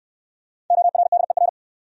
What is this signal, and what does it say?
Morse code, heard as a keyed tone, about 62 wpm, 700 Hz. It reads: ORDER